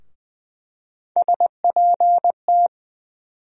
Morse code, heard as a keyed tone, about 20 wpm, 700 Hz. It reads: SPT
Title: USA Call Sign Postfixes - 3 Letters 20wpm